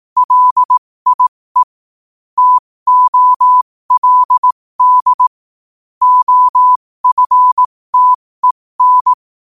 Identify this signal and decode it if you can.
Morse code, heard as a keyed tone, about 18 wpm, spaced as for 15 wpm, 1000 Hz. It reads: LIE TOLD OFTEN
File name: qotdfiles / 15wpm